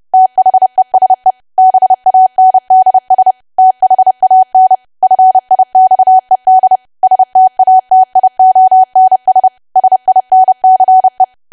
Morse code, heard as a keyed tone, about 30 wpm, 750 Hz. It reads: THESE BANDS THAN FIXED STATIONS SINCE
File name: cwa / W1AW-Level3 / 30WPM